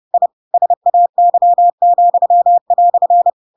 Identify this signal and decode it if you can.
Morse code, heard as a keyed tone, about 30 wpm, 700 Hz. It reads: I SAY,"
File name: Pinocchio / Pinocchio ch7 30 30 ik1hge